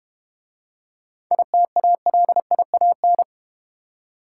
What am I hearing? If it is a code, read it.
Morse code, heard as a keyed tone, about 31 wpm, 700 Hz. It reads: ITALIAN